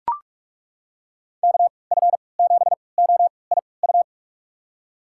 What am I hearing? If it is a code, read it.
Morse code, heard as a keyed tone, about 43 wpm, spaced as for 25 wpm, 700 Hz. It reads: KF7OIU
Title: USA Call Signs - 6 Characters - 3x Character Spacing - 45wpm